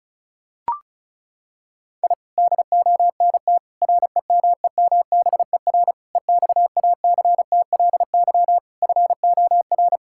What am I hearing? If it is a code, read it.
Morse code, heard as a keyed tone, about 34 wpm, 700 Hz. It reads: I DONT REMEMBER EXACTLY FOR
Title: Sentences from Top 1000 Words - Review Part 6 - 35wpm